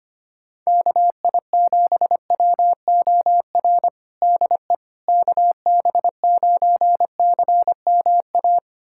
Morse code, heard as a keyed tone, about 25 wpm, 700 Hz. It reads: KI7WOR DE KB9CMA
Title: USA Callsign DE Callsign - 6 Characters - 25wpm